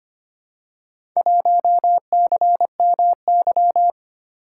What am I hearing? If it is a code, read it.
Morse code, heard as a keyed tone, about 25 wpm, 700 Hz. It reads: 1CMY
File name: practice / 25wpm / Four Letter-Number Combinations - 25wpm